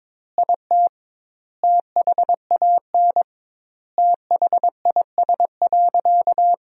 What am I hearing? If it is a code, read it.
Morse code, heard as a keyed tone, about 21 wpm, 700 Hz. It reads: IT THAN THIS.
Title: 2x Word-Spacing - Sentences from Top 100 Words - 22wpm